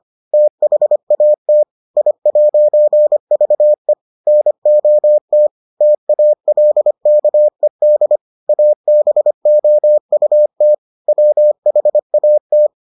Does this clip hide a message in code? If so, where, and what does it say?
Yes: Morse code from 0.3 to 12.7 s, heard as a keyed tone, about 25 wpm, 600 Hz. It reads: THAT I'VE NOT TALKED ABOUT WHAT